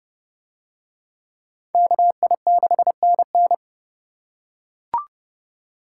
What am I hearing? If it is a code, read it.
Morse code, heard as a keyed tone, about 31 wpm, 700 Hz. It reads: KI6NN